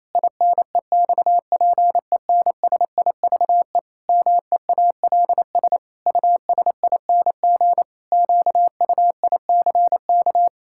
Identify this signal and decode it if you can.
Morse code, heard as a keyed tone, about 28 wpm, 700 Hz. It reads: INEXPENSIVE MEALS USING QUICK
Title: Pangram Challenge v2 - 28wpm